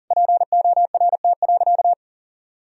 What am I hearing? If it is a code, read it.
Morse code, heard as a keyed tone, about 40 wpm, 700 Hz. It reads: PORT.
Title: Sentences from Top 1000 Words - Review Part 1 - 40wpm